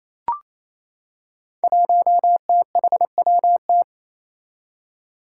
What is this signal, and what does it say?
Morse code, heard as a keyed tone, about 28 wpm, 700 Hz. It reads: 1THWT
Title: Five Challenging Letter-Number Combinations 28wpm